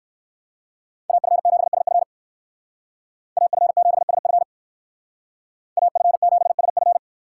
Morse code, heard as a keyed tone, about 59 wpm, 700 Hz. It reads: AF7SF AF7SF AF7SF